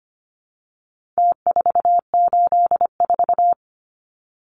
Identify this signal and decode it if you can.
Morse code, heard as a keyed tone, about 25 wpm, 700 Hz. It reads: T484
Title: Numbers - Contest Style - 4-Digit 25wpm